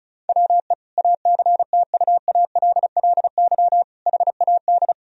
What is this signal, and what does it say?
Morse code, heard as a keyed tone, about 34 wpm, 700 Hz. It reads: WE ACTUALLY HAD
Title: Sentences from Top 1000 Words - Review Part 4 - 35wpm